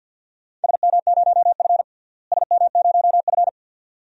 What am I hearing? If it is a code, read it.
Morse code, heard as a keyed tone, about 52 wpm, 700 Hz. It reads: SM0F SM0F
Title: DXCC-100 Call Signs - 4 Characters - Repeated 2x - 50wpm